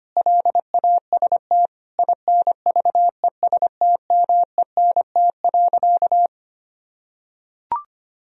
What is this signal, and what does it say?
Morse code, heard as a keyed tone, about 25 wpm, 700 Hz. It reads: LAST INVESTMENT.
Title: Sentences from Top 1000 Words - Review Part 4 - 25wpm